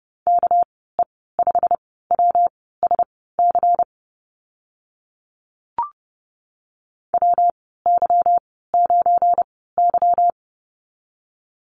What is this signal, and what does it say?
Morse code, heard as a keyed tone, about 30 wpm, spaced as for 17 wpm, 700 Hz. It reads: KE5WSC WY9Y
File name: Lesson 37g - Letter x - Call Signs 30wpm